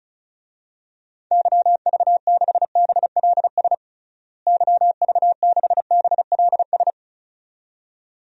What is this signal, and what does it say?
Morse code, heard as a keyed tone, about 36 wpm, 700 Hz. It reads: YV6BLS YV6BLS
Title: DXCC-100 Call Signs - 6 Characters - Repeated 2x - 35wpm